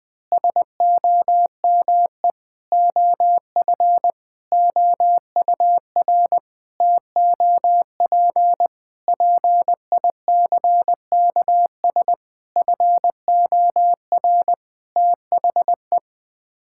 Morse code, heard as a keyed tone, about 21 wpm, 700 Hz. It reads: SOME OF OUR TOP PICKS FOR THE
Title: Sets of 9 Words - 20wpm